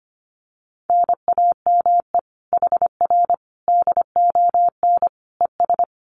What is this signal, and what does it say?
Morse code, heard as a keyed tone, about 25 wpm, 700 Hz. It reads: NAME HR DON ES
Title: Common QSO Phrases Encore 25wpm